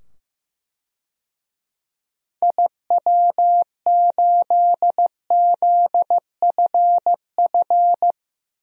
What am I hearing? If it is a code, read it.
Morse code, heard as a keyed tone, about 16 wpm, 700 Hz. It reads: IW8ZFF
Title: DXCC-100 Call Signs - 6 Characters - 15wpm